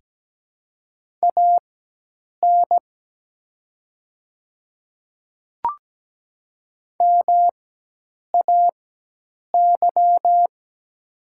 Morse code, heard as a keyed tone, about 17 wpm, spaced as for 8 wpm, 700 Hz. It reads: AN MAY
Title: 4x Character Spacing - Top 100 Words - 17wpm